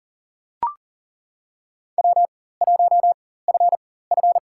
Morse code, heard as a keyed tone, about 42 wpm, spaced as for 19 wpm, 700 Hz. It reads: W1FF
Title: USA Call Signs - 4 Characters - 4x Character Spacing - 40wpm